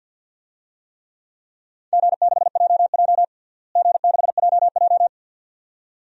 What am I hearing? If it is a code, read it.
Morse code, heard as a keyed tone, about 52 wpm, 700 Hz. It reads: G6JJ G6JJ
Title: DXCC-100 Call Signs - 4 Characters - Repeated 2x - 50wpm